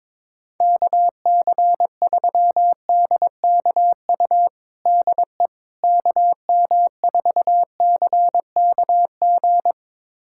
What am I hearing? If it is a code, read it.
Morse code, heard as a keyed tone, about 22 wpm, 700 Hz. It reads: KC3DKU DE KM4CKG
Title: USA Callsign DE Callsign 22wpm